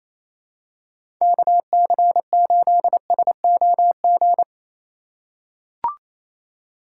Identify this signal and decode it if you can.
Morse code, heard as a keyed tone, about 28 wpm, 700 Hz. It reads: KC8SOG
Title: USA Call Signs - 2x Character Spacing - 28wpm